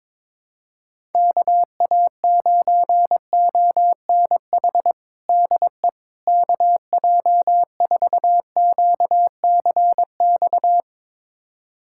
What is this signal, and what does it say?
Morse code, heard as a keyed tone, about 22 wpm, 700 Hz. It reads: KA9ONH DE KJ4QCX
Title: USA Callsign DE Callsign - 6 Characters - 22wpm